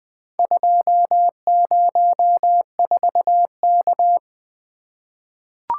Morse code, heard as a keyed tone, about 20 wpm, 700 Hz. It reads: 204K
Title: Four Letter-Number Combinations - 20wpm